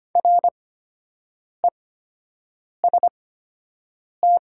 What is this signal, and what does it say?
Morse code, heard as a keyed tone, about 25 wpm, spaced as for 7 wpm, 700 Hz. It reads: REST
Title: 8x Character Spacing - Top 500 Words - 25wpm